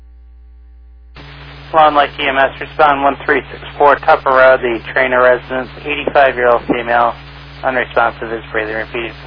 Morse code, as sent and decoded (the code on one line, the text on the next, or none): none